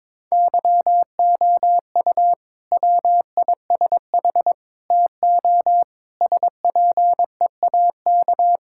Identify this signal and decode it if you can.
Morse code, heard as a keyed tone, about 21 wpm, 700 Hz. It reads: YOU WISH TO SPEAK
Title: Sentences from Top 1600 Words 22wpm